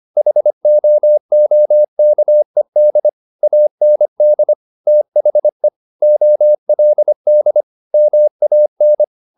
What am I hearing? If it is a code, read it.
Morse code, heard as a keyed tone, about 25 wpm, 600 Hz. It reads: HOOKED AND THE OLD MAN